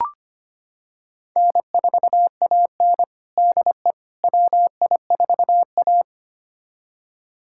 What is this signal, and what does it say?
Morse code, heard as a keyed tone, about 25 wpm, 700 Hz. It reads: N4AN DE WI4A
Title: USA Callsign DE Callsign - 4 Characters - 25wpm